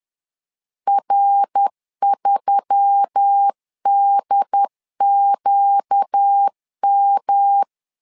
Morse code, heard as a keyed tone, about 10 wpm, 800 Hz. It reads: R3DQM
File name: LEZIONI / 4 Q1S3 10wpm 2427